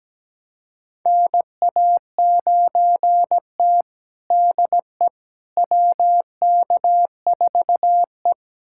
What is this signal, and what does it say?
Morse code, heard as a keyed tone, about 17 wpm, 700 Hz. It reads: NA9T DE WK4E
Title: USA Callsign DE Callsign - 4 Characters - 17wpm